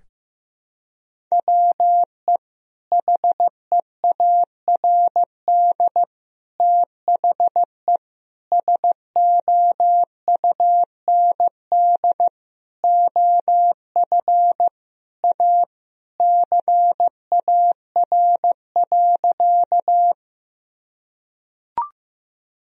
Morse code, heard as a keyed tone, about 16 wpm, 700 Hz. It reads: WE HEARD THE SOUND OF A CAR.
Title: Sentences from Top 1000 Words - Review Part 1 - 15wpm